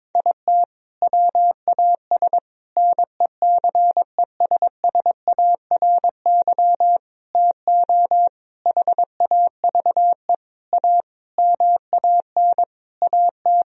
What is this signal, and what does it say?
Morse code, heard as a keyed tone, about 22 wpm, 700 Hz. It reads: IT WAS NECESSARY TO HAVE A MAN AT
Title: Sentences from Top 1000 Words - Review Part 1 - 22wpm